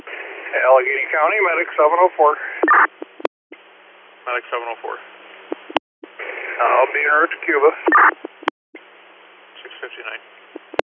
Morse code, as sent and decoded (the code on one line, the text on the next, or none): none